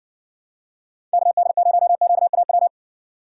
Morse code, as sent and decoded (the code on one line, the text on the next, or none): -.- -.. ----- --.- .- .--
KD0QAW